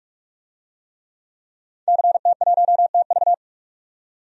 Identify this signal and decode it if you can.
Morse code, heard as a keyed tone, about 46 wpm, 700 Hz. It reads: CT1TV
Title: DXCC-100 Call Signs - 5 Characters - 45wpm